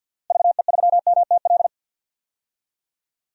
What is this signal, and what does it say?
Morse code, heard as a keyed tone, about 51 wpm, 700 Hz. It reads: VE3MTL
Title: DXCC-100 Call Signs - 6 Characters - Repeated 2x - 50wpm